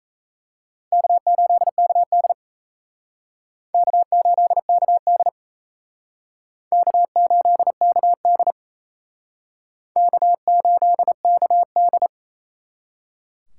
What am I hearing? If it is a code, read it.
Morse code, heard as a keyed tone, about 35 wpm, 700 Hz. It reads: K8KD K8KD K8KD K8KD
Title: Speed Racing - USA Call Signs 28wpm